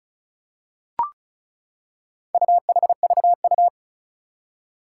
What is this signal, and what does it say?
Morse code, heard as a keyed tone, about 36 wpm, 700 Hz. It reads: UHVU
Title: Four Challenging Letter-Number Combinations 35wpm